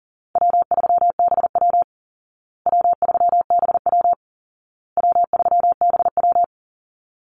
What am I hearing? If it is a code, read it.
Morse code, heard as a keyed tone, about 40 wpm, 700 Hz. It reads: W3BW W3BW W3BW
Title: Easy Call Signs Repeated 3x - 40wpm